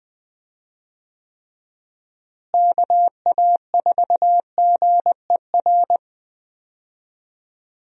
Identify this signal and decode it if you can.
Morse code, heard as a keyed tone, about 20 wpm, 700 Hz. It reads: KA4GER
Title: USA Call Signs - 6 Characters - Repeated 2x - 20wpm